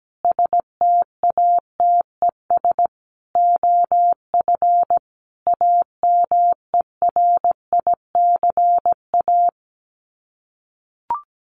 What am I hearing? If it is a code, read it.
Morse code, heard as a keyed tone, about 17 wpm, 700 Hz. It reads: STATES OF AMERICA
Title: Sets of 5 Words 17wpm